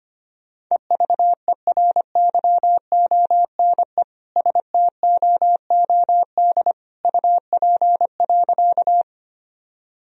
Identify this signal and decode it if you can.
Morse code, heard as a keyed tone, about 25 wpm, 700 Hz. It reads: EVERYONE STOOD UP.
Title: Sentences from Top 1000 Words - Review Part 2 - 25wpm